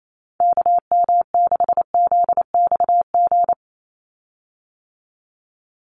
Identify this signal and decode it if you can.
Morse code, heard as a keyed tone, about 28 wpm, 700 Hz. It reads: KM6ZXG
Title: USA Call Signs 28wpm